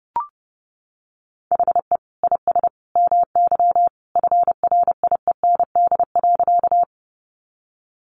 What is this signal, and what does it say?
Morse code, heard as a keyed tone, about 30 wpm, 700 Hz. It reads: HE IS MY FRIEND.
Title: Sentences from Top 300 Words 30wpm